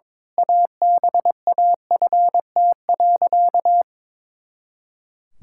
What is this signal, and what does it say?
Morse code, heard as a keyed tone, about 21 wpm, 700 Hz. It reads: ABAFT.
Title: Pangram Challenge v2 - 22wpm